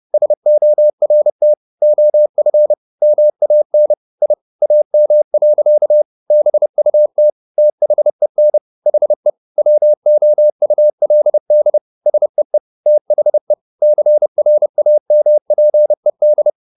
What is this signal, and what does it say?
Morse code, heard as a keyed tone, about 30 wpm, 600 Hz. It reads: SORT OF MAN I AM. BUT THEN HE WOULD SEE THE CRAMPED
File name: the old man and the sea / the old man and the sea 1 30wpm